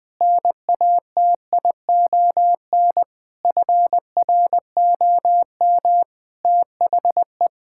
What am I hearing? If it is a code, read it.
Morse code, heard as a keyed tone, about 20 wpm, 700 Hz. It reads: NATION FROM THE
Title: Sentences with 5-Word Phrases - 20wpm